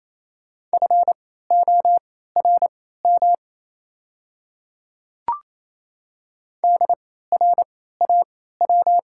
Morse code, heard as a keyed tone, about 27 wpm, spaced as for 16 wpm, 690 Hz. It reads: FORM DRAW
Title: 3x Character Spacing - Top 500 Words - 28wpm